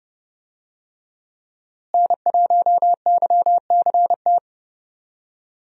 Morse code, heard as a keyed tone, about 30 wpm, 700 Hz. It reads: N1YCT
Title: USA Call Signs - 5 Characters - Repeated 3x - 30wpm